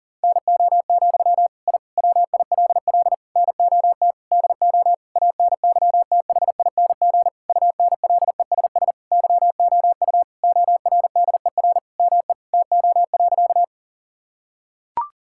Morse code, heard as a keyed tone, about 40 wpm, 700 Hz. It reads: NO, I WILL NOT DO ANYTHING UNLESS YOU ORDER ME TO.
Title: Sentences from Top 1000 Words - Review Part 6 - 40wpm